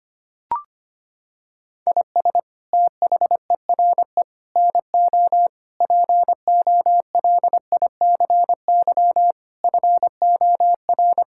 Morse code, heard as a keyed tone, about 25 wpm, 700 Hz. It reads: IS THERE NO POLICY FOR
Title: Sentences from Top 1000 Words - Review Part 5 - 25wpm